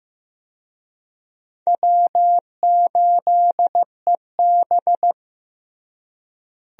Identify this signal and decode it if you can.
Morse code, heard as a keyed tone, about 15 wpm, 700 Hz. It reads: W8EB
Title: USA Call Signs - 4 Characters - Repeated 3x - 15wpm